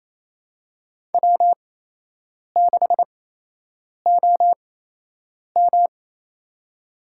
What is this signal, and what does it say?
Morse code, heard as a keyed tone, about 28 wpm, spaced as for 8 wpm, 700 Hz. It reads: W6OM
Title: USA Call Signs - 4 Characters - 8x Character Spacing - 28wpm